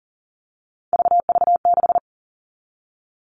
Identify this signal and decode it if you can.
Morse code, heard as a keyed tone, about 40 wpm, 700 Hz. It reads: VV6
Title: All Cut Numbers - 3-Digit 40wpm